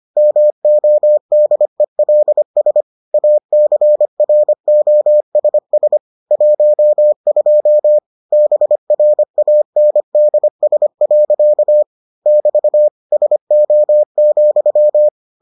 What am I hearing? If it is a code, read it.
Morse code, heard as a keyed tone, about 25 wpm, 600 Hz. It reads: MODELS ACROSS 12 BRANDS. <BT> SO,